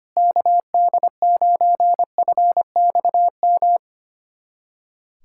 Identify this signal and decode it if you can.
Morse code, heard as a keyed tone, about 25 wpm, 700 Hz. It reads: KD9FXM
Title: USA Callsign DE Callsign 25wpm